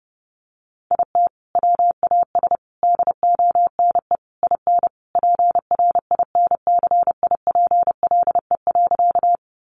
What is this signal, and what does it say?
Morse code, heard as a keyed tone, about 30 wpm, 700 Hz. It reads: IT WAS DONE IN PRINCIPLE.
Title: Sentences from Top 1400 Words 30wpm